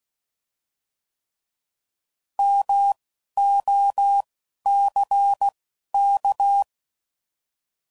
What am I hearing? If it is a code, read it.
Morse code, heard as a keyed tone, about 16 wpm, spaced as for 11 wpm, 800 Hz. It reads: MOCK